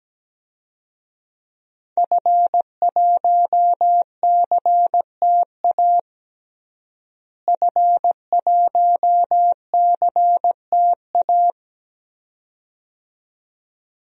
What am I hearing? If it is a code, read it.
Morse code, heard as a keyed tone, about 17 wpm, 700 Hz. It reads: F1CTA F1CTA
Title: DXCC-100 Call Signs - 5 Characters - Repeated 2x - 17wpm